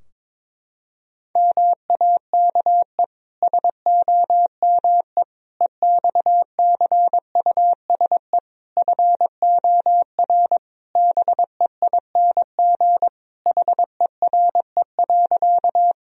Morse code, heard as a keyed tone, about 22 wpm, 700 Hz. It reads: MAKE SOME EXCUSE FOR BEING HERE.
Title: Sentences from Top 1000 Words - Review Part 3 - 22wpm